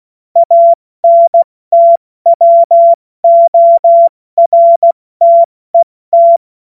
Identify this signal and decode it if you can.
Morse code, heard as a keyed tone, about 16 wpm, spaced as for 14 wpm, 680 Hz. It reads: ANTWORTET
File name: Morsekurs / Witze / Witz 13. September 2022 70 F80